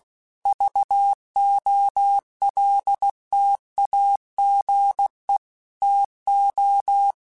Morse code, heard as a keyed tone, about 16 wpm, 800 Hz. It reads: VOLTAGE TO
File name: Technical Subjects / Technical Subjects-mp3s / XC @15